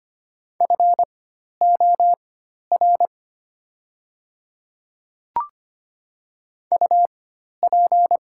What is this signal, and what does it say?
Morse code, heard as a keyed tone, about 25 wpm, spaced as for 12 wpm, 690 Hz. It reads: FOR UP
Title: 4x Character Spacing - Top 100 Words - 25wpm